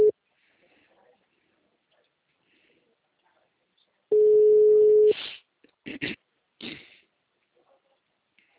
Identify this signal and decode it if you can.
Ringback tone (Europe, 425 Hz).